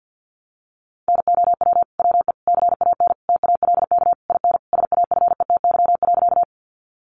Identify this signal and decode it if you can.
Morse code, heard as a keyed tone, about 50 wpm, 700 Hz. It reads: NOW WE CAN TALK IN SAFETY.
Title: Sentences from Top 700 Words 50wpm